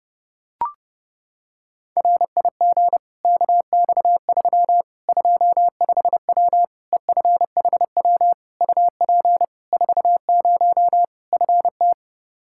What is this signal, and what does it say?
Morse code, heard as a keyed tone, about 32 wpm, 700 Hz. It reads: RIG KX3 25W EFHW UP 40 FT
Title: Practice QSO - Exchange 2 - Info - 30wpm